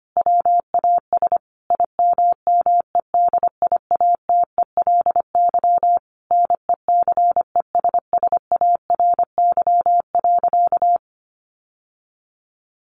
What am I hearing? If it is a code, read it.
Morse code, heard as a keyed tone, about 25 wpm, 700 Hz. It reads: WAS IMMEDIATELY NECESSARY.
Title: Sentences from Top 1100 Words 25wpm